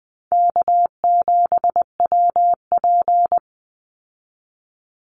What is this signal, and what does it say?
Morse code, heard as a keyed tone, about 20 wpm, 700 Hz. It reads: K7WP
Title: Easy Call Signs Repeated 2x - 20wpm